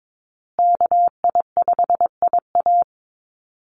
Morse code, heard as a keyed tone, about 22 wpm, 700 Hz. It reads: KI5IA
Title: USA Call Signs Repeated 3x 22wpm